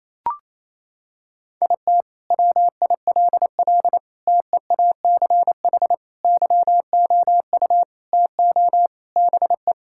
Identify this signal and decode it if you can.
Morse code, heard as a keyed tone, about 28 wpm, 700 Hz. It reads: IT WILL TEACH YOU TO BE
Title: Sentences from Top 1600 Words 28wpm